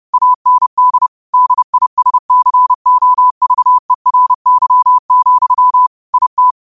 Morse code, heard as a keyed tone, about 30 wpm, 1000 Hz. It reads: AND DISCOVERY, IT
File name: 30wpm